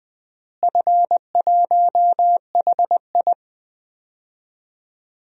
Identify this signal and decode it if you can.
Morse code, heard as a keyed tone, about 20 wpm, 700 Hz. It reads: F1HI
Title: DXCC-100 Call Signs - 4 Characters - Repeated 3x - 20wpm